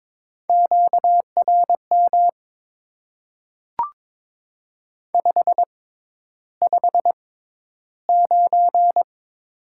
Morse code, heard as a keyed tone, about 22 wpm, spaced as for 11 wpm, 700 Hz. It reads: QRM 559
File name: Top 90 QSO Elements - 6x Character Spacing - 22wpm